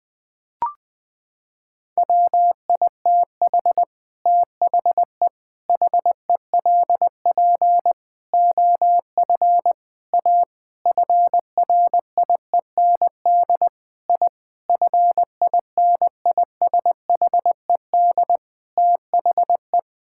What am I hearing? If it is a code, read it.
Morse code, heard as a keyed tone, about 20 wpm, 700 Hz. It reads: WITH THE HELP OF A FRIEND I FINISHED THE
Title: Sentences with 5-Word Phrases - 20wpm